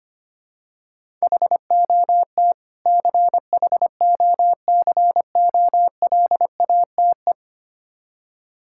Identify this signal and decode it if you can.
Morse code, heard as a keyed tone, about 25 wpm, 700 Hz. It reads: HOT CHOCOLATE
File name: practice / 25wpm / Difficult Words with Hints - 25wpm